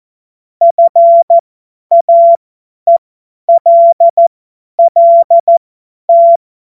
Morse code, heard as a keyed tone, about 14 wpm, spaced as for 10 wpm, 680 Hz. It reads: FAELLT
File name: Morsekurs / Witze / Witz 31.Oktober 2023 50 F70